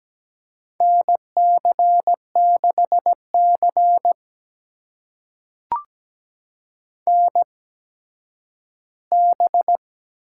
Morse code, heard as a keyed tone, about 17 wpm, spaced as for 10 wpm, 700 Hz. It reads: NC6C NB
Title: USA Call Signs - 4 Characters - 8x Character Spacing - 17wpm